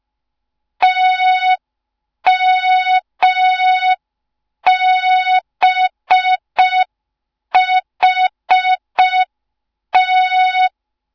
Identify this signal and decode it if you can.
Morse code, heard as a keyed tone, about 5 wpm, 3700 Hz. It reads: TMBHT
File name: Morze-HA2MN / hang / 07 GYAKORLAT